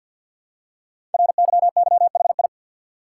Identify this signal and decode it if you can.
Morse code, heard as a keyed tone, about 48 wpm, 700 Hz. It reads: RYYHI
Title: Code Groups - 5 Letters - 50wpm